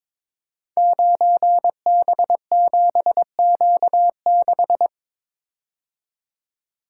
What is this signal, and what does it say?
Morse code, heard as a keyed tone, about 22 wpm, 700 Hz. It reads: 9B7Q6
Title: Code Groups - 5 Letter:Numbers - 22wpm